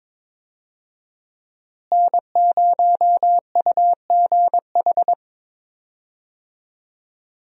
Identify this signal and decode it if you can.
Morse code, heard as a keyed tone, about 22 wpm, 700 Hz. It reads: N0UGH
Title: USA Call Signs - 5 Characters - Repeated 3x - 22wpm